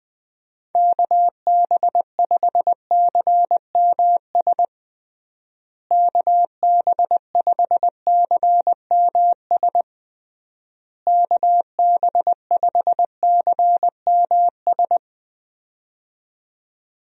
Morse code, heard as a keyed tone, about 20 wpm, 700 Hz. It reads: KB5CMS KB5CMS KB5CMS
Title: USA Call Signs - 6 Characters - Repeated 3x - 20wpm